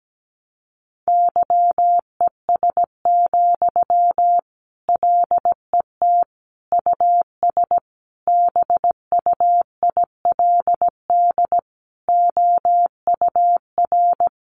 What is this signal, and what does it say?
Morse code, heard as a keyed tone, about 17 wpm, 700 Hz. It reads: YES, LET US BUILD OUR
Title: Sentences from Top 1200 Words 17wpm